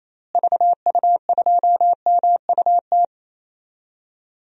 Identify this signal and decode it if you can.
Morse code, heard as a keyed tone, about 27 wpm, 700 Hz. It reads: VU2MUT
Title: DXCC-100 Call Signs - 6 Characters - Repeated 2x - 28wpm